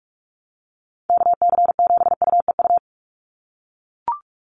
Morse code, heard as a keyed tone, about 44 wpm, 700 Hz. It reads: KC7UEU
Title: USA Call Signs 45wpm